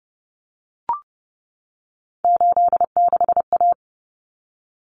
Morse code, heard as a keyed tone, about 30 wpm, 700 Hz. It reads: 86A
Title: Common Cut Numbers - 3-Digit 30wpm